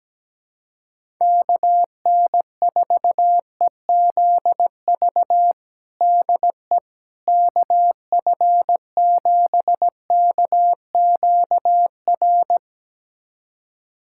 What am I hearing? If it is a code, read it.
Morse code, heard as a keyed tone, about 17 wpm, 700 Hz. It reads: KN4EZV DE KF7KQR
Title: USA Callsign DE Callsign 17wpm